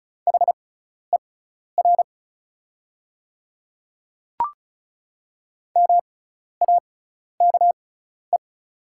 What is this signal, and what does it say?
Morse code, heard as a keyed tone, about 36 wpm, spaced as for 12 wpm, 700 Hz. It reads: HER MAKE
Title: 6x Character Spacing - Top 100 Words - 35wpm